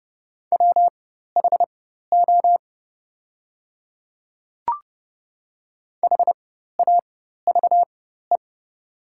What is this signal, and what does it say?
Morse code, heard as a keyed tone, about 30 wpm, spaced as for 14 wpm, 700 Hz. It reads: WHO HAVE
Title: 4x Character Spacing - Top 100 Words - 30wpm